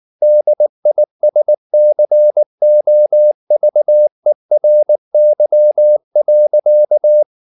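Morse code, heard as a keyed tone, about 19 wpm, 600 Hz. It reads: DISCOVERY.
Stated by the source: Morse code at 19 wpm from CW med Gnister, reading Ken Follett's novel Never in English.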